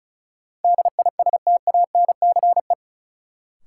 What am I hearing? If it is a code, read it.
Morse code, heard as a keyed tone, about 36 wpm, 700 Hz. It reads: DISTANCE
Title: Sentences from Top 1200 Words - New Word Drill - 35wpm